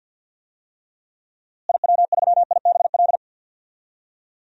Morse code, heard as a keyed tone, about 52 wpm, 700 Hz. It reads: IW3IBL
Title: DXCC-100 Call Signs - 6 Characters - Repeated 3x - 50wpm